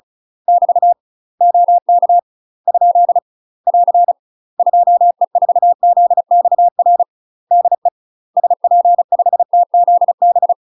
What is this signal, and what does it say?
Morse code, heard as a keyed tone, about 34 wpm, 700 Hz. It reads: <BT> OK ? <AR> 2E4ZXR DE SP5TZB
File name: UK MASTER QSOs TRAINING / 35WPM / UK MASTER QSOs 0024